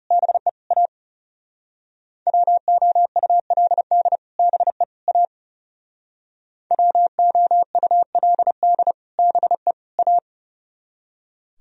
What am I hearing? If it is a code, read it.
Morse code, heard as a keyed tone, about 37 wpm, 700 Hz. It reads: BE A WOULD BE A WOULD BE A
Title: Speed-Racing - Sets of 3 Words 30wpm